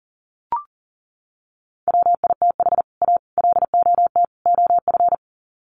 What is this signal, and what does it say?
Morse code, heard as a keyed tone, about 40 wpm, 700 Hz. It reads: WITH A LOT OF
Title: Sets of 4 Words 40wpm